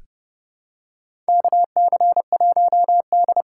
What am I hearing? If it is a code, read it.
Morse code, heard as a keyed tone, about 30 wpm, 700 Hz. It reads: KC1D
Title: USA Call Signs - 6 Characters - 8x Character Spacing - 30wpm